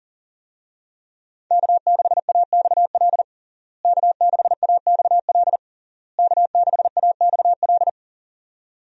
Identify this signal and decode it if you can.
Morse code, heard as a keyed tone, about 37 wpm, 700 Hz. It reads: K6AXL K6AXL K6AXL
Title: USA Call Signs - 5 Characters - Repeated 3x - 40wpm